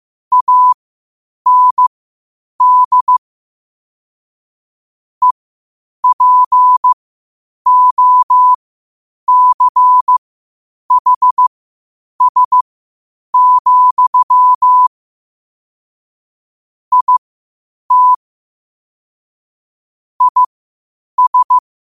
Morse code, heard as a keyed tone, about 15 wpm, spaced as for 8 wpm, 1000 Hz. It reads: AND EPOCHS, IT IS
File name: qotdfiles / 8wpm